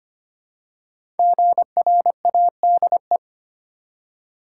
Morse code, heard as a keyed tone, about 25 wpm, 700 Hz. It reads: GRADE
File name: 5 Letter Words - First 3 Letters Given - 25wpm